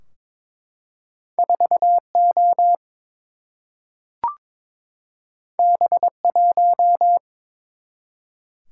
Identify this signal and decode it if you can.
Morse code, heard as a keyed tone, about 23 wpm, 700 Hz. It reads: 4O B1
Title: Single-Letter Single-Number 22wpm